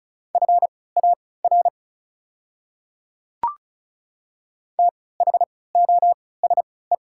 Morse code, heard as a keyed tone, about 36 wpm, spaced as for 20 wpm, 700 Hz. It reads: FAR THOSE
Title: 3x Character Spacing - Top 300 Words - 35wpm